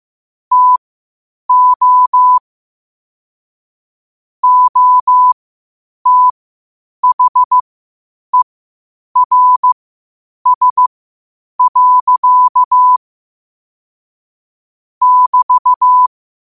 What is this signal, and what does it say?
Morse code, heard as a keyed tone, about 15 wpm, spaced as for 8 wpm, 1000 Hz. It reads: TO OTHERS. <BT>
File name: qotdfiles / 8wpm